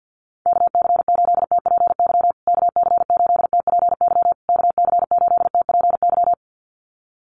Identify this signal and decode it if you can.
Morse code, heard as a keyed tone, about 50 wpm, 700 Hz. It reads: KC8TPY KC8TPY KC8TPY
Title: USA Call Signs Repeated 3x 50wpm